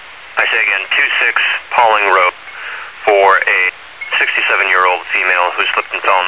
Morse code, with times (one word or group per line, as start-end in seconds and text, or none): none